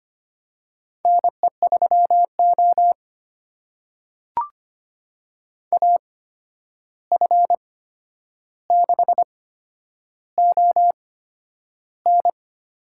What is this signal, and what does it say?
Morse code, heard as a keyed tone, about 25 wpm, spaced as for 10 wpm, 700 Hz. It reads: NE3O AF6ON
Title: USA Call Signs - 8x Character Spacing - 25wpm